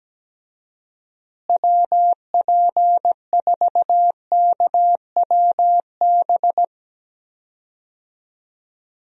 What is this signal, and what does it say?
Morse code, heard as a keyed tone, about 17 wpm, 700 Hz. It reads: WP4KWB
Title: USA Call Signs - 6 Characters - Repeated 3x - 17wpm